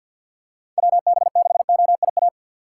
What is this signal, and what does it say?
Morse code, heard as a keyed tone, about 52 wpm, 700 Hz. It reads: WB6OIA
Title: USA Call Signs - 6x Character Spacing - 50wpm